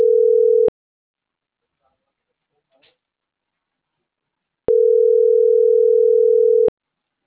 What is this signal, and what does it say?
Ringback tone (North America), 440 + 480 Hz.